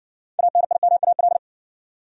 Morse code, heard as a keyed tone, about 60 wpm, 700 Hz. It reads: ANIMAL